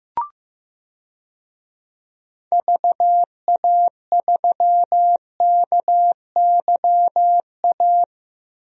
Morse code, heard as a keyed tone, about 15 wpm, 700 Hz. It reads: VA3KYA
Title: DXCC-100 Call Signs - 6 Characters - 15wpm